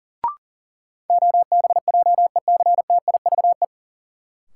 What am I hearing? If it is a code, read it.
Morse code, heard as a keyed tone, about 40 wpm, 700 Hz. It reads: OBJECTIVE
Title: Sentences from Top 1200 Words - New Word Drill Alt - 40wpm